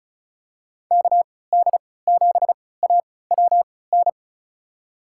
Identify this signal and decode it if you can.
Morse code, heard as a keyed tone, about 36 wpm, spaced as for 20 wpm, 690 Hz. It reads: KD7AWN
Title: USA Call Signs - 6 Characters - 3x Character Spacing - 35wpm